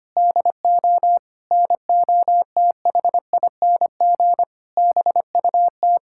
Morse code, heard as a keyed tone, about 25 wpm, 700 Hz. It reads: DO NOTHING BUT
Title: Sentences from Top 1600 Words 25wpm